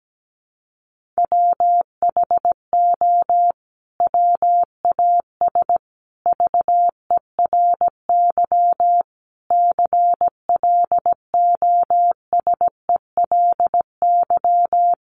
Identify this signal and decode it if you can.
Morse code, heard as a keyed tone, about 17 wpm, 700 Hz. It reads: WHO WAS VERY CLOSELY
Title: Sentences from Top 1400 Words 17wpm